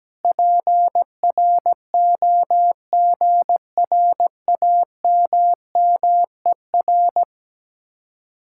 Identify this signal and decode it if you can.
Morse code, heard as a keyed tone, about 17 wpm, 700 Hz. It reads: PROGRAMMER
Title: Speed-Racing Common QSO Phrases Encore - 17wpm